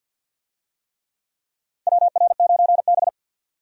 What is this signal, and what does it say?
Morse code, heard as a keyed tone, about 51 wpm, 700 Hz. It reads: WR9B
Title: USA Call Signs - 4 Characters - 50wpm